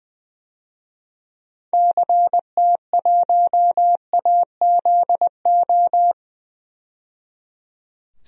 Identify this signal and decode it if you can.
Morse code, heard as a keyed tone, about 20 wpm, 700 Hz. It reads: CT1AZO
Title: DXCC-100 Call Signs - 6 Characters - 20wpm